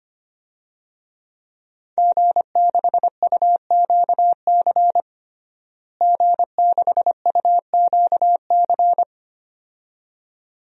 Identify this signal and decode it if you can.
Morse code, heard as a keyed tone, about 25 wpm, 700 Hz. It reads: G6UQC G6UQC
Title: DXCC-100 Call Signs - 5 Characters - Repeated 2x - 25wpm